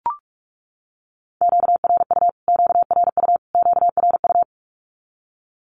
Morse code, heard as a keyed tone, about 44 wpm, 700 Hz. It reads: QRU QRU QRU
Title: Q-Codes Repeated 3x 45wpm